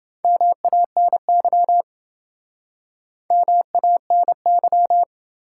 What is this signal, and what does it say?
Morse code, heard as a keyed tone, about 29 wpm, 700 Hz. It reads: MANY MANY